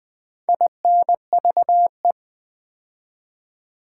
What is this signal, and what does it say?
Morse code, heard as a keyed tone, about 20 wpm, 700 Hz. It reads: INVE
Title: Build A Word - 2 Letters At A Time - 8-Letter Words - 20wpm